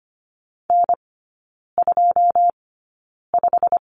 Morse code, heard as a keyed tone, about 25 wpm, spaced as for 9 wpm, 700 Hz. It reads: N25
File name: Lesson 15e - Number 5 - Three Characters 25wpm